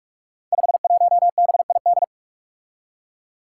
Morse code, heard as a keyed tone, about 44 wpm, 700 Hz. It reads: 51BID
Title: Five Challenging Letter-Number Combinations 45wpm